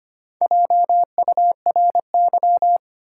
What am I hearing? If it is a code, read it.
Morse code, heard as a keyed tone, about 25 wpm, 700 Hz. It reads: JURY
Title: Four Letter Words - 25wpm